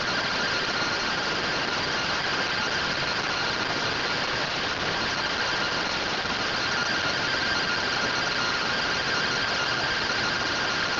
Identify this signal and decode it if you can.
Two-tone teleprinter signal, about 170 Hz shift.
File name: findings / MR AXIAL 33,99mm 1.gen-41 rtty rx